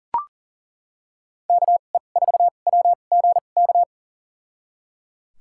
Morse code, heard as a keyed tone, about 37 wpm, spaced as for 28 wpm, 700 Hz. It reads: KE4WGK
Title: USA Call Signs - 6 Characters - 2x Character Spacing - 40wpm